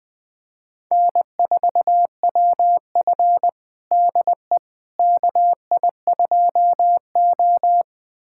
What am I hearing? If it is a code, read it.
Morse code, heard as a keyed tone, about 20 wpm, 700 Hz. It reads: N4WF DE KI2O
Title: USA Callsign DE Callsign - 4 Characters - 20wpm